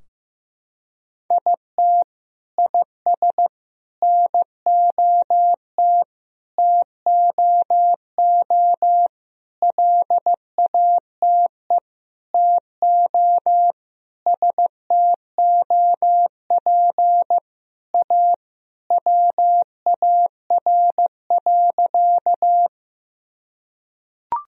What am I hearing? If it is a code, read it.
Morse code, heard as a keyed tone, about 15 wpm, 700 Hz. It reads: IT IS NOT TOO LATE TO STOP A WAR.
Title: Sentences from Top 1000 Words - Review Part 2 - 15wpm